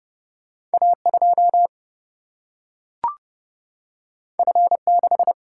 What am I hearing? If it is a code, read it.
Morse code, heard as a keyed tone, about 30 wpm, 700 Hz. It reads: A 2 F 6
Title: Single-Letter Single-Number 30wpm